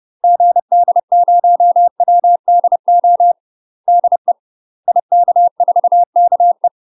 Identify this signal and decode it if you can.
Morse code, heard as a keyed tone, about 30 wpm, 700 Hz. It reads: GD0WDO DE IK4KE
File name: UK MASTER QSOs TRAINING / 30WPM / UK MASTER QSOs 0013